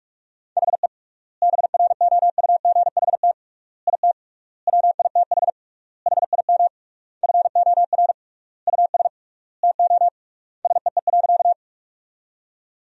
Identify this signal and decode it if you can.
Morse code, heard as a keyed tone, about 46 wpm, 700 Hz. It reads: HE BROUGHT IT WITH HIM FOR US TO SEE.
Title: 3x Word-Spacing - Sentences from Top 500 Words - 45wpm